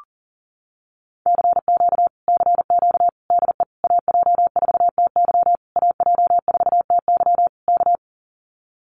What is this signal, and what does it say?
Morse code, heard as a keyed tone, about 40 wpm, 700 Hz. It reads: CQ CQ DE AJ4TY AJ4TY K
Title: Common QSO Phrases Encore 40wpm